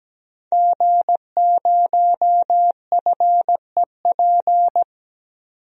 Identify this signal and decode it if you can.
Morse code, heard as a keyed tone, about 17 wpm, 700 Hz. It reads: G0FEP
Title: DXCC-100 Call Signs - 5 Characters - 17wpm